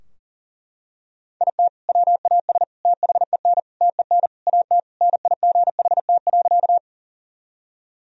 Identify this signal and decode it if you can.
Morse code, heard as a keyed tone, about 37 wpm, 700 Hz. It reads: IT WAS THEN TEN AT NIGHT.
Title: Sentences from Top 1000 Words - Review Part 2 - 40wpm